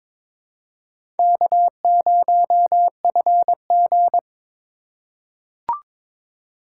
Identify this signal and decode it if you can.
Morse code, heard as a keyed tone, about 23 wpm, 700 Hz. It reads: K0FG
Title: USA Call Signs - 4 Characters - 2x Character Spacing - 22wpm